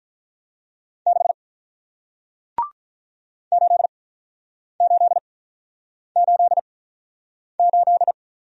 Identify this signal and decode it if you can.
Morse code, heard as a keyed tone, about 47 wpm, spaced as for 9 wpm, 700 Hz. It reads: 6 8888